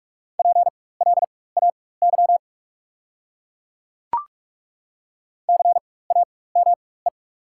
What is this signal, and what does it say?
Morse code, heard as a keyed tone, about 41 wpm, spaced as for 20 wpm, 700 Hz. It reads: PLAY CAME
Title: 4x Character Spacing - Top 300 Words - 45wpm